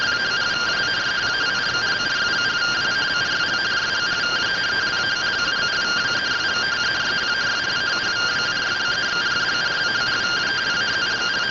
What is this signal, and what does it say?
Two-tone teleprinter signal, about 170 Hz shift.